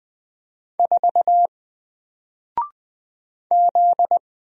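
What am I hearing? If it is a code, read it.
Morse code, heard as a keyed tone, about 20 wpm, 700 Hz. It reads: 4 Z
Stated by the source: Single letter and number Morse drill at 20 wpm, from N4LCD.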